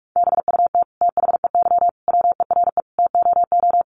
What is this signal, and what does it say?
Morse code, heard as a keyed tone, about 44 wpm, 700 Hz. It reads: BUT THEY WERE TOO
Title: Sentences from Top 1500 Words 45wpm